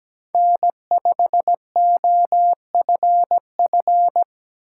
Morse code, heard as a keyed tone, about 17 wpm, 700 Hz. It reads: N5OFF
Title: USA Call Signs - 8x Character Spacing - 17wpm